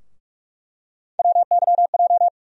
Morse code, heard as a keyed tone, about 46 wpm, 700 Hz. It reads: WYJ
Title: USA Call Sign Postfixes - 3 Letters 45wpm